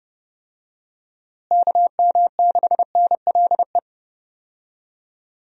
Morse code, heard as a keyed tone, about 30 wpm, 700 Hz. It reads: KM6NLE
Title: USA Call Signs - 6 Characters - Repeated 3x - 30wpm